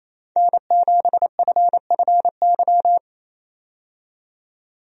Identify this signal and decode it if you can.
Morse code, heard as a keyed tone, about 28 wpm, 700 Hz. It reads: N7FFY
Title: USA Call Signs - 5 Characters - 28wpm